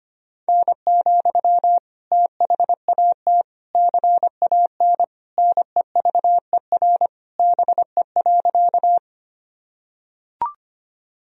Morse code, heard as a keyed tone, about 25 wpm, 700 Hz. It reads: N, THAT CAN NEVER BE.
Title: Sentences from Top 300 Words and Abbreviations 25wpm